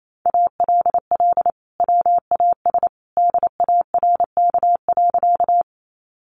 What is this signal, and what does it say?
Morse code, heard as a keyed tone, about 28 wpm, 700 Hz. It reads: ALL WAS DARK.
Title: Sentences from Top 500 Words 28wpm